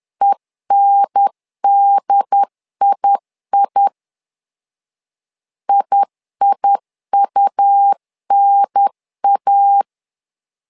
Morse code, heard as a keyed tone, about 10 wpm, 800 Hz. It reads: ENDII IIUNA
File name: LEZIONI / 2 UHDN / 2 UHDN 10wpm 1904